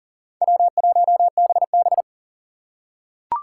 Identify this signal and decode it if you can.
Morse code, heard as a keyed tone, about 40 wpm, 700 Hz. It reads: W1BB
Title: USA Call Signs - 4 Characters - 4x Character Spacing - 40wpm